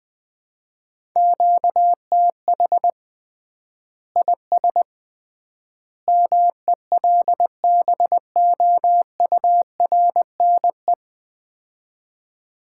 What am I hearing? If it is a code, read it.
Morse code, heard as a keyed tone, about 20 wpm, 700 Hz. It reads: QTH IS MELBOURNE
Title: Common QSO Phrases - 3x Word-Spacing - 20wpm